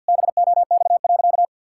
Morse code, heard as a keyed tone, about 50 wpm, 700 Hz. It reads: BOX.
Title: Sentences from Top 1600 Words 50wpm